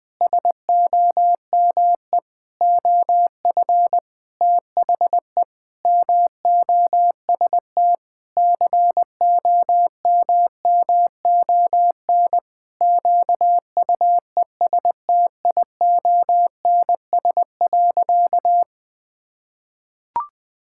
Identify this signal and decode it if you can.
Morse code, heard as a keyed tone, about 20 wpm, 700 Hz. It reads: SOME OF THE MOST COMMON QUESTIONS.
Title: Sentences with 4-Word Phrases - 20wpm